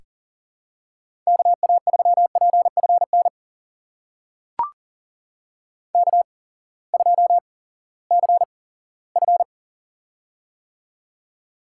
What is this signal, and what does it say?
Morse code, heard as a keyed tone, about 40 wpm, spaced as for 20 wpm, 700 Hz. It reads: KA3PFN K2CF
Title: USA Call Signs - 8x Character Spacing - 40wpm